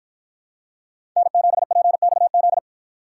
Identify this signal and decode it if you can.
Morse code, heard as a keyed tone, about 53 wpm, 700 Hz. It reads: N7PXZ